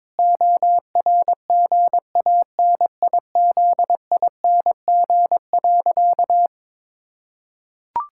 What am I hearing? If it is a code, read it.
Morse code, heard as a keyed tone, about 22 wpm, 700 Hz. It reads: ORGANIZING.
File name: Sentences with 7-Word Phrases - 22wpm